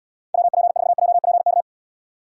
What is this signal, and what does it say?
Morse code, heard as a keyed tone, about 79 wpm, 700 Hz. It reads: WP4JPK